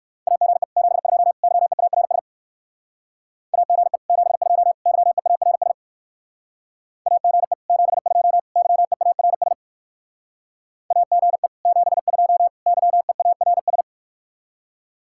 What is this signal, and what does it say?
Morse code, heard as a keyed tone, about 60 wpm, 700 Hz. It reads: AGE 72 YEARS AGE 72 YEARS AGE 72 YEARS AGE 72 YEARS